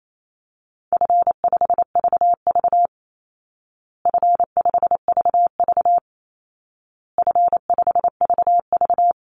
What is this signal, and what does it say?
Morse code, heard as a keyed tone, about 28 wpm, 700 Hz. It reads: F5VV F5VV F5VV
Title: Easy Call Signs Repeated 3x - 28wpm